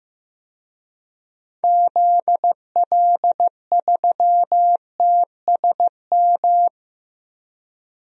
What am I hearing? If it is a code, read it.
Morse code, heard as a keyed tone, about 15 wpm, 700 Hz. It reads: ZL3TSM
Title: DXCC-100 Call Signs - 6 Characters - 15wpm